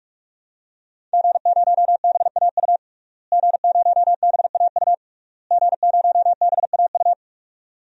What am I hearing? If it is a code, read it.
Morse code, heard as a keyed tone, about 46 wpm, 700 Hz. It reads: G0BAU G0BAU G0BAU
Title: DXCC-100 Call Signs - 5 Characters - Repeated 3x - 45wpm